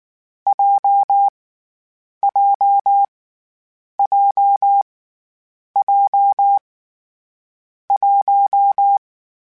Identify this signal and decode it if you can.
Morse code, heard as a keyed tone, about 19 wpm, 800 Hz. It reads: J J J J 1
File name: Difficult Combos / Difficult Combos-mp3s / WPJ WPJ1 @18